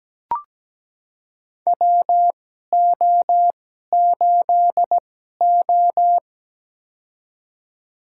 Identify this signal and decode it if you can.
Morse code, heard as a keyed tone, about 17 wpm, spaced as for 12 wpm, 700 Hz. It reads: WO8O
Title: USA Call Signs - 4 Characters - 2x Character Spacing - 17wpm